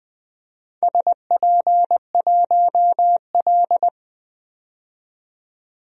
Morse code, heard as a keyed tone, about 20 wpm, 700 Hz. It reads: SP1L
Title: DXCC-100 Call Signs - 4 Characters - 20wpm